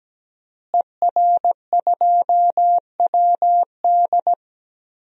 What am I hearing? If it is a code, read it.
Morse code, heard as a keyed tone, about 17 wpm, 700 Hz. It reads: ER2WD
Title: DXCC-100 Call Signs - 5 Characters - Repeated 3x - 17wpm